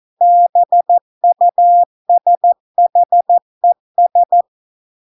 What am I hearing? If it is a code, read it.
Morse code, heard as a keyed tone, about 14 wpm, 700 Hz. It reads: BUSHES